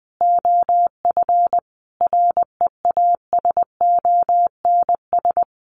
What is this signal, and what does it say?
Morse code, heard as a keyed tone, about 20 wpm, 700 Hz. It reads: OF REASONS
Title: Sets of 5 Words Encore 20wpm